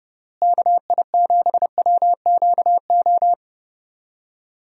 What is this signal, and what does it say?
Morse code, heard as a keyed tone, about 30 wpm, 700 Hz. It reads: KI7WQO
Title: USA Call Signs - 6 Characters - Repeated 3x - 30wpm